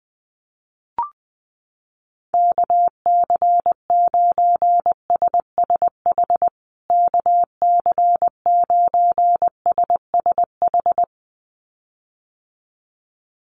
Morse code, heard as a keyed tone, about 20 wpm, 700 Hz. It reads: KC9SSH KC9SSH
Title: USA Call Signs Repeated 2x 20wpm